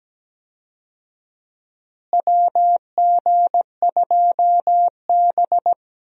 Morse code, heard as a keyed tone, about 17 wpm, 700 Hz. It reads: WG2B
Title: USA Call Signs - 4 Characters - Repeated 2x - 17wpm